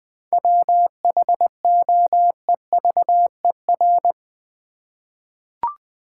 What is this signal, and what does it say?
Morse code, heard as a keyed tone, about 20 wpm, 700 Hz. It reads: WHOEVER
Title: Seven Letter Words - 20wpm